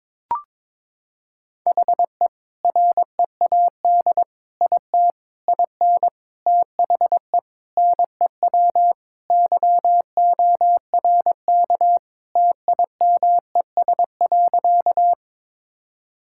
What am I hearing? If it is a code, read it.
Morse code, heard as a keyed tone, about 22 wpm, 700 Hz. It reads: HE READ IT IN THE NEW YORK TIMES.
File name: practice / 22wpm / Sentences with 4-Word Phrases - 22wpm